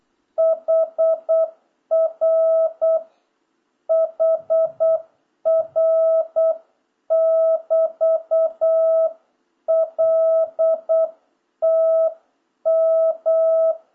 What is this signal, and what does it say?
Morse code, heard as a keyed tone, about 8 wpm, 650 Hz. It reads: HR HR <BT> L T M